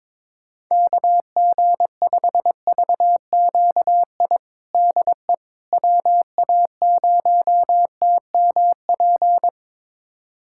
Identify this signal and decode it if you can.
Morse code, heard as a keyed tone, about 22 wpm, 700 Hz. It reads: KG5VQI DE WA0TMP
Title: USA Callsign DE Callsign 22wpm